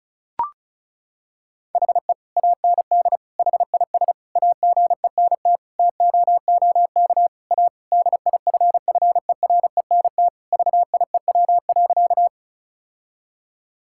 Morse code, heard as a keyed tone, about 36 wpm, 700 Hz. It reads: HE AND HIS AGENT TOOK A DIFFERENT VIEW.
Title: Sentences from Top 1600 Words 35wpm